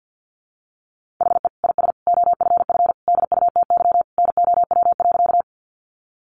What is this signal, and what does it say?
Morse code, heard as a keyed tone, about 50 wpm, 700 Hz. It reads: HE IS OFF DUTY NOW.
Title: Sentences from Top 1500 Words 50wpm